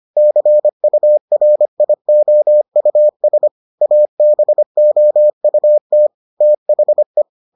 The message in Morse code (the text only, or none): CURIOUS ABOUT THE